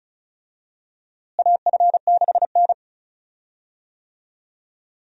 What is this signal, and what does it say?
Morse code, heard as a keyed tone, about 34 wpm, 700 Hz. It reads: AF6N
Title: USA Call Signs - 4 Characters - Repeated 2x - 35wpm